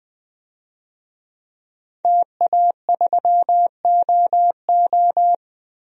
Morse code, heard as a keyed tone, about 20 wpm, 700 Hz. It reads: TA3OO
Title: DXCC-100 Call Signs - 5 Characters - 20wpm